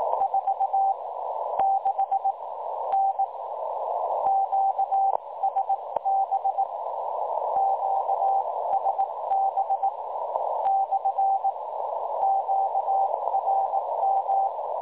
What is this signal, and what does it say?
Morse code, heard as a keyed tone, about 19 wpm, spaced as for 14 wpm, 780 Hz, with noise. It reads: 46N QSB QSB / QRM
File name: dlazaawansowanych / nagraniazpasma / websdr recording 3565.0kHz